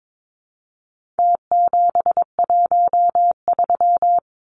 Morse code, heard as a keyed tone, about 22 wpm, 700 Hz. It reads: T713
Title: Numbers - Contest Style - 4-Digit 22wpm